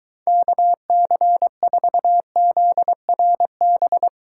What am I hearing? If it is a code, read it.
Morse code, heard as a keyed tone, about 23 wpm, 700 Hz. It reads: KC4ZRB